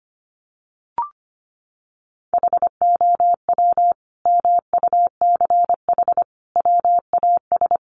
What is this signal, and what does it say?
Morse code, heard as a keyed tone, about 25 wpm, 700 Hz. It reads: HOW MUCH WAS
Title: Sentences from Top 1400 Words 25wpm